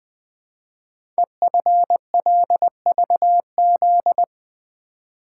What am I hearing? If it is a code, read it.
Morse code, heard as a keyed tone, about 20 wpm, 700 Hz. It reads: EFLVZ
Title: Code Groups - 5 Letters - 20wpm